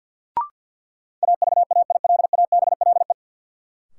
Morse code, heard as a keyed tone, about 46 wpm, 700 Hz. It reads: AVAILABLE
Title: Sentences from Top 600 Words - New Word Drill - 50wpm